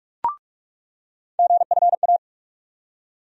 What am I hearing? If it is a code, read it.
Morse code, heard as a keyed tone, about 46 wpm, 700 Hz. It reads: GFA
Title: USA Call Sign Postfixes - 3 Letters 45wpm